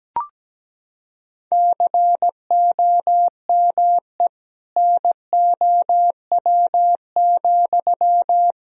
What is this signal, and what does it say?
Morse code, heard as a keyed tone, about 17 wpm, 700 Hz. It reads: COME NOW,
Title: Sentences from Top 1000 Words - Review Part 2 - 17wpm